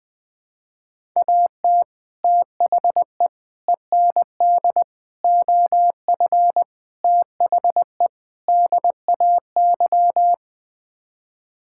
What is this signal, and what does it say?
Morse code, heard as a keyed tone, about 20 wpm, 700 Hz. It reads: AT THE END OF THE DAY
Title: Sets of 6 Words - 20wpm